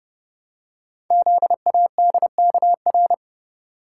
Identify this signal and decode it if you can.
Morse code, heard as a keyed tone, about 30 wpm, 700 Hz. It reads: ZADKR
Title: Code Groups - 5 Letters - 30wpm